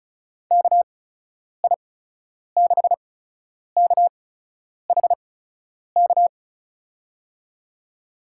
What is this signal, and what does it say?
Morse code, heard as a keyed tone, about 36 wpm, spaced as for 10 wpm, 700 Hz. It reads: KI6KHK
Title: USA Call Signs - 6 Characters - 8x Character Spacing - 35wpm